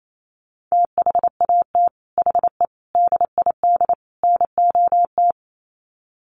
Morse code, heard as a keyed tone, about 28 wpm, 700 Hz. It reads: THAT HE DID NOT
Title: Sets of 4 Words 28wpm